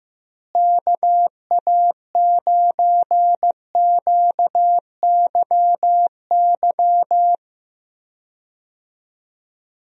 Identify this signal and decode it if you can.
Morse code, heard as a keyed tone, about 15 wpm, 700 Hz. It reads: KA9QYY
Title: USA Call Signs - 6 Characters - Repeated 2x - 15wpm